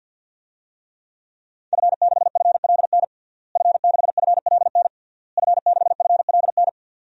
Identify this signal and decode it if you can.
Morse code, heard as a keyed tone, about 49 wpm, 700 Hz. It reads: F6FLN F6FLN F6FLN
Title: DXCC-100 Call Signs - 5 Characters - Repeated 3x - 50wpm